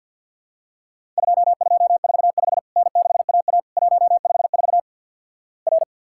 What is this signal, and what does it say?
Morse code, heard as a keyed tone, about 50 wpm, 700 Hz. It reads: 2245 N6AA 154 R
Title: Worked All Europe DX Contest - QTC Exchange - 50wpm